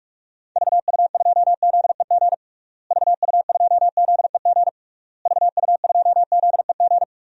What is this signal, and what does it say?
Morse code, heard as a keyed tone, about 46 wpm, 700 Hz. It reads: VU2ZEG VU2ZEG VU2ZEG
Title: DXCC-100 Call Signs - 6 Characters - Repeated 3x - 45wpm